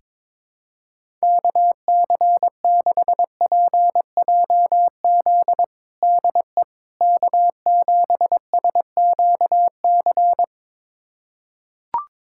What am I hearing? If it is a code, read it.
Morse code, heard as a keyed tone, about 22 wpm, 700 Hz. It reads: KC6PJZ DE K7SQC
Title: USA Callsign DE Callsign 22wpm